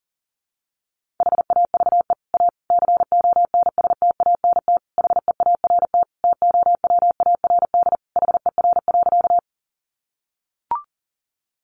Morse code, heard as a keyed tone, about 40 wpm, 700 Hz. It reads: HAVE A CONSTANT HEART TOWARD HER.
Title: Sentences from Top 1500 Words 40wpm